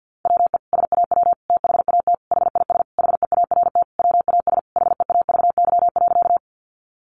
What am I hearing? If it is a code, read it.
Morse code, heard as a keyed tone, about 50 wpm, 700 Hz. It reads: WE SAW THAT HIS HEART WAS HEAVY.
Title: Sentences from Top 1200 Words 50wpm